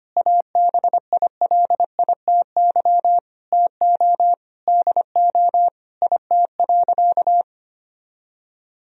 Morse code, heard as a keyed tone, about 25 wpm, 700 Hz. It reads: ABILITY TO DO IT.
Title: Sentences from Top 1000 Words - Review Part 3 - 25wpm